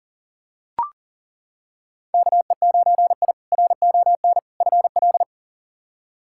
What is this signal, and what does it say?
Morse code, heard as a keyed tone, about 37 wpm, 700 Hz. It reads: KE9I RON FL
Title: K1USN SST QSO Practice - 40wpm